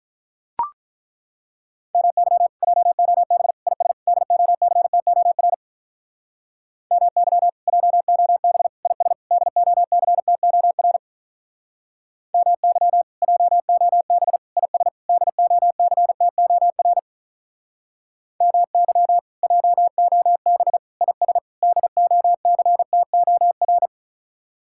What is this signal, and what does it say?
Morse code, heard as a keyed tone, about 42 wpm, 700 Hz. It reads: MY JOB IS DOCTOR MY JOB IS DOCTOR MY JOB IS DOCTOR MY JOB IS DOCTOR
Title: Speed-Racing Common QSO Phrases - 35wpm